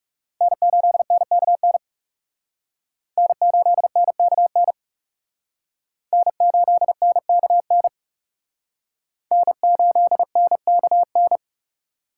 Morse code, heard as a keyed tone, about 39 wpm, 700 Hz. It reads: N8NKN N8NKN N8NKN N8NKN